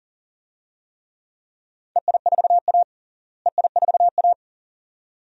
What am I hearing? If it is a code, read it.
Morse code, heard as a keyed tone, about 40 wpm, 700 Hz. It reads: EI4A EI4A
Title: DXCC-100 Call Signs - 4 Characters - Repeated 2x - 40wpm